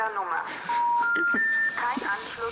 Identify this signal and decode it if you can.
Special information tone, with speech.
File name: Analog rotary phones get a second life with raspberry pi